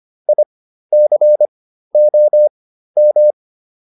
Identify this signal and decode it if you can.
Morse code, heard as a keyed tone, about 25 wpm, spaced as for 13 wpm, 600 Hz. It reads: ICOM
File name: QSO 210 13